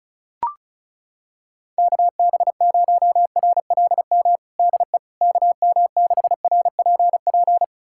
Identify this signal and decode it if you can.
Morse code, heard as a keyed tone, about 34 wpm, 700 Hz. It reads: KB0RLM DE KM6RPP
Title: USA Callsign DE Callsign - 6 Characters - 35wpm